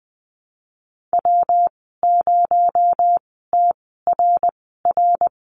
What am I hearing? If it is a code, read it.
Morse code, heard as a keyed tone, about 20 wpm, spaced as for 14 wpm, 700 Hz. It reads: W0TRR
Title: 238 Lesson 38g - Letter q - Call Signs 20wpm